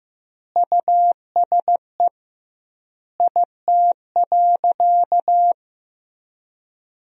Morse code, heard as a keyed tone, about 15 wpm, 700 Hz. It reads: USE IT.
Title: 2x Word-Spacing - Sentences from Top 500 Words - 15wpm